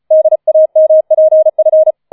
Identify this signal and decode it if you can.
Morse code, heard as a keyed tone, about 34 wpm, 630 Hz. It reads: DAMPF
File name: 004 Vierter Streich 35wpm